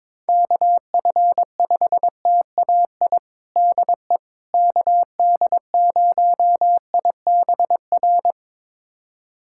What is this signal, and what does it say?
Morse code, heard as a keyed tone, about 22 wpm, 700 Hz. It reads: KF5TAI DE KD0IBR
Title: USA Callsign DE Callsign 22wpm